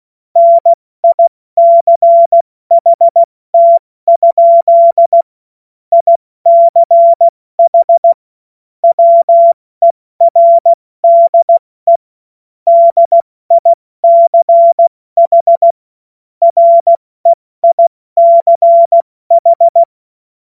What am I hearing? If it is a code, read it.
Morse code, heard as a keyed tone, about 16 wpm, 680 Hz. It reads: NICHT? ICH WERDE DICH REICH
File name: Morsekurs / Witze / Witz 08. November 2022 70 F80